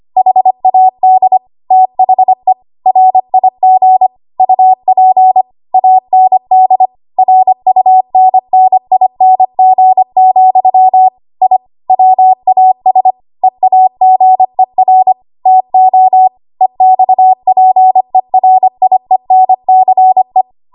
Morse code, heard as a keyed tone, about 25 wpm, 750 Hz. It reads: HAD THE RIG UP AND RUNNING, I WAS EAGER TO EXPERIENCE
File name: Archive / 25 WPM / 25WPM